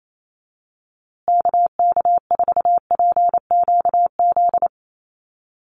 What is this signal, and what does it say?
Morse code, heard as a keyed tone, about 28 wpm, 700 Hz. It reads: KK4PQZ
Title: USA Call Signs Repeated 2x 28wpm